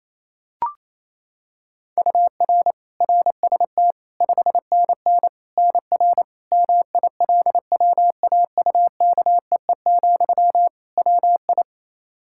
Morse code, heard as a keyed tone, about 28 wpm, 700 Hz. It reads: UR RST 5NN NR MILWAUKEE, WI
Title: Practice QSO - Exchange 1 - Beginning - 28wpm